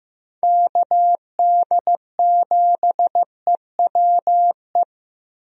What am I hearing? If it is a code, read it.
Morse code, heard as a keyed tone, about 15 wpm, 700 Hz. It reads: KD7EWE
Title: USA Call Signs - 6 Characters - Repeated 3x - 15wpm